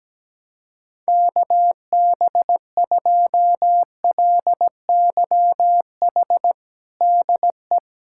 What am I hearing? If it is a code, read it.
Morse code, heard as a keyed tone, about 17 wpm, 700 Hz. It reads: KB2LYH DE
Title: USA Callsign DE Callsign 17wpm